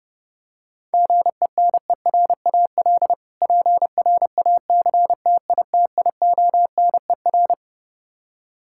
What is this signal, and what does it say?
Morse code, heard as a keyed tone, about 30 wpm, 700 Hz. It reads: GENERAL PRACTITIONER
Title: Practice QSO - Exchange 3 - More Info - 30wpm